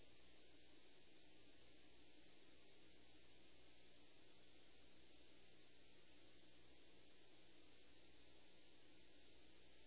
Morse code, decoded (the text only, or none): none